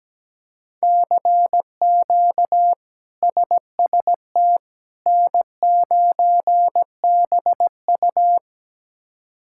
Morse code, heard as a keyed tone, about 17 wpm, 700 Hz. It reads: CQ SST N9BU
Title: K1USN SST QSO Practice - 17wpm